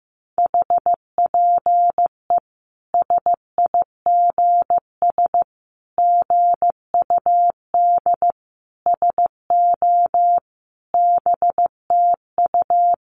Morse code, heard as a keyed tone, about 15 wpm, 700 Hz. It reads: HPE SIGS GUD SO BTU
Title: Common QSO Phrases 15wpm